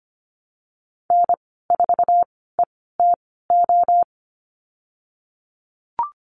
Morse code, heard as a keyed tone, about 25 wpm, spaced as for 16 wpm, 700 Hz. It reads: N4ETO
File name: Lesson 15g - Number 5 - Call Signs 25wpm